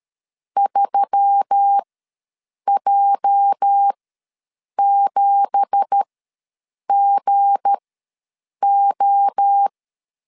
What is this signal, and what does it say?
Morse code, heard as a keyed tone, about 13 wpm, spaced as for 7 wpm, 800 Hz. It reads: 3J7GO
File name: LEZIONI / 9 2867 ws 12wpm 3431